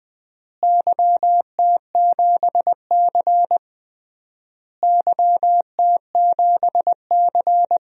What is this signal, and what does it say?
Morse code, heard as a keyed tone, about 20 wpm, 700 Hz. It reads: YT7C YT7C
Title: DXCC-100 Call Signs - 4 Characters - Repeated 3x - 20wpm